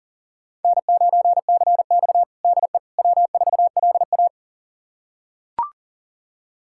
Morse code, heard as a keyed tone, about 37 wpm, 700 Hz. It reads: N9CX DE W4LA
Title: USA Callsign DE Callsign - 4 Characters - 40wpm